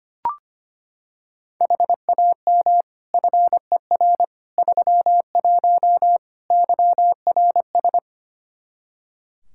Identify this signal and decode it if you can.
Morse code, heard as a keyed tone, about 25 wpm, 700 Hz. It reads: HAM FER 31 YRS
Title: Practice QSO - Exchange 3 - More Info - 25wpm